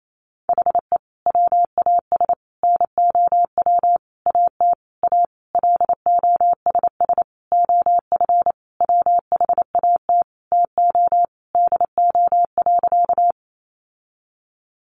Morse code, heard as a keyed tone, about 28 wpm, 700 Hz. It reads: HE WAS NOW AT A LOSS OF WHAT TO DO.
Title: Sentences from Top 1400 Words 28wpm